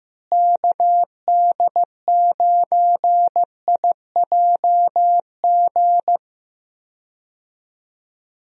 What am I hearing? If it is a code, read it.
Morse code, heard as a keyed tone, about 15 wpm, 700 Hz. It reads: KD9IJG
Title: USA Call Signs - 6 Characters - Repeated 3x - 15wpm